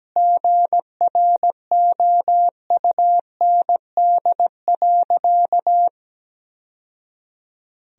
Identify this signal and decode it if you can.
Morse code, heard as a keyed tone, about 17 wpm, 700 Hz. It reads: GROUND.
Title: Sentences from Top 1000 Words - Review Part 3 - 17wpm